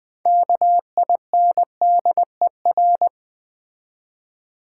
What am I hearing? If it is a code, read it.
Morse code, heard as a keyed tone, about 20 wpm, 700 Hz. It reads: KINDER
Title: Build A Word - 6 Letters At A Time - 12-Letter Words - 20wpm